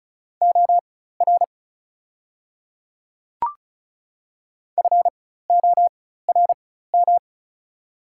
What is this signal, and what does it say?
Morse code, heard as a keyed tone, about 34 wpm, spaced as for 16 wpm, 690 Hz. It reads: OR FORM
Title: 4x Character Spacing - Top 200 Words - 35wpm